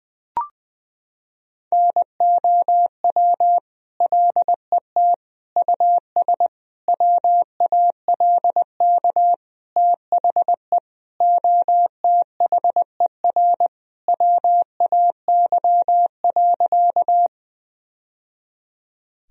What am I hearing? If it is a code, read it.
Morse code, heard as a keyed tone, about 20 wpm, 700 Hz. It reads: NOW LET US WALK THE OTHER WAY.
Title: Sentences from Top 1000 Words - Review Part 1 - 20wpm